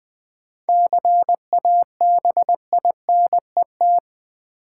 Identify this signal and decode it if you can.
Morse code, heard as a keyed tone, about 21 wpm, 700 Hz. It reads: CABINET
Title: Seven Letter Words - 20wpm